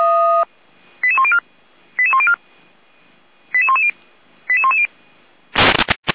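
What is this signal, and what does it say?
Five-tone selective call (ZVEI), heard as quick runs of five tones, tones of about 80 ms. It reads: 88104 88104 88110 88110
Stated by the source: ZVEI five-tone call 88405 received on an SDR.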